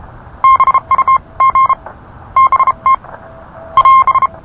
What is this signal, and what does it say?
Morse code, heard as a keyed tone, about 33 wpm, spaced as for 27 wpm, 1060 Hz. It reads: BUG BT <AS>